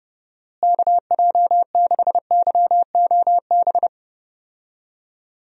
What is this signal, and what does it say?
Morse code, heard as a keyed tone, about 30 wpm, 700 Hz. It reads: KJ6YOB
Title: USA Call Signs - 6 Characters - Repeated 2x - 30wpm